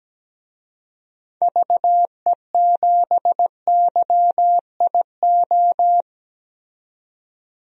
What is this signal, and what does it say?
Morse code, heard as a keyed tone, about 17 wpm, 700 Hz. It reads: VE7YIO
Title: DXCC-100 Call Signs - 6 Characters - 17wpm